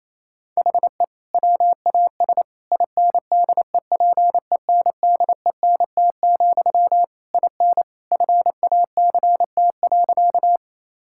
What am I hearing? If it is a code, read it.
Morse code, heard as a keyed tone, about 27 wpm, 700 Hz. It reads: HE WAS INDEPENDENT, IN FACT.
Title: Sentences from Top 1000 Words - Review Part 6 - 28wpm